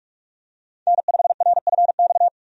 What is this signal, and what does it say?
Morse code, heard as a keyed tone, about 46 wpm, 700 Hz. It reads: N5RFX
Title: USA Call Signs - 5 Characters - 45wpm